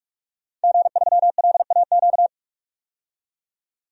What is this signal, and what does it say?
Morse code, heard as a keyed tone, about 44 wpm, 690 Hz. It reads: G3LAQ
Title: DXCC-100 Call Signs - 5 Characters - 45wpm